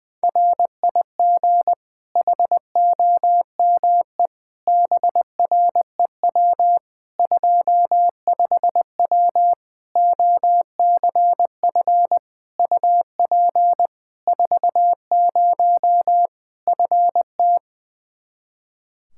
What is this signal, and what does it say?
Morse code, heard as a keyed tone, about 20 wpm, 700 Hz. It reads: RIG HOME BREW 25W OCF UP 40 FT
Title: Practice QSO - Exchange 2 - Info - 20wpm